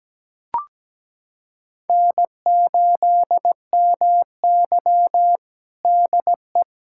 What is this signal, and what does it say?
Morse code, heard as a keyed tone, about 17 wpm, 700 Hz. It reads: N8MY DE
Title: USA Callsign DE Callsign - 4 Characters - 17wpm